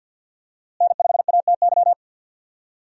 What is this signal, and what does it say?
Morse code, heard as a keyed tone, about 48 wpm, 700 Hz. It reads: N5ATY
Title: USA Call Signs - 5 Characters - Repeated 2x - 50wpm